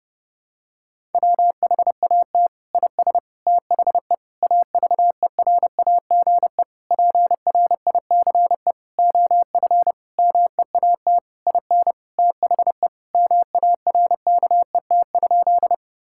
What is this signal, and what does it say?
Morse code, heard as a keyed tone, about 30 wpm, 700 Hz. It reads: WHAT IS THE AVERAGE PRICE OF MEAT IN THE MARKET?
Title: Sentences from Top 1000 Words - Review Part 6 - 30wpm